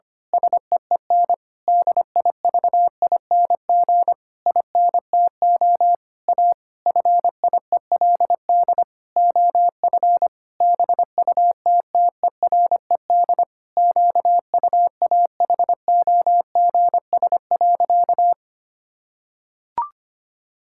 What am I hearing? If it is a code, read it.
Morse code, heard as a keyed tone, about 25 wpm, 700 Hz. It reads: SEEN DIVING INTO A FIELD OF BUTTERED QUAHOGS.
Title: Pangram Challenge v2 - 25wpm